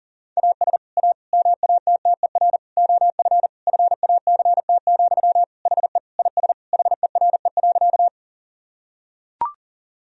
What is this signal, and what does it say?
Morse code, heard as a keyed tone, about 40 wpm, 700 Hz. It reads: AS A MATTER OF FACT, HE IS HERE.
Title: Sentences from Top 1000 Words - Review Part 3 - 40wpm